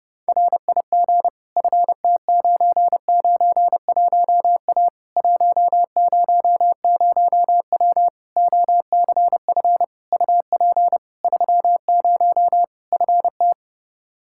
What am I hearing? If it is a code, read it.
Morse code, heard as a keyed tone, about 30 wpm, 700 Hz. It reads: RIG FT991A 100W OCF UP 30 FT
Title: Practice QSO - Exchange 2 - Info - 30wpm